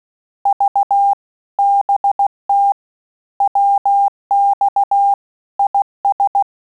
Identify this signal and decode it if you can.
Morse code, heard as a keyed tone, about 16 wpm, 800 Hz. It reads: V BT WX IS
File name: QSO CW 75 CPM